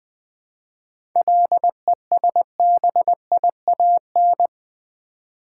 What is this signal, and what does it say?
Morse code, heard as a keyed tone, about 20 wpm, 700 Hz. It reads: LESBIAN
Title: 7 Letter Words - First 3 Letters Given - 20wpm